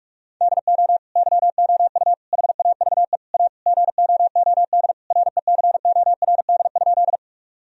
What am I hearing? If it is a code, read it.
Morse code, heard as a keyed tone, about 46 wpm, 700 Hz. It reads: DO YOU HAVE A GOOD RECORD?
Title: Sentences from Top 1000 Words - Review Part 3 - 45wpm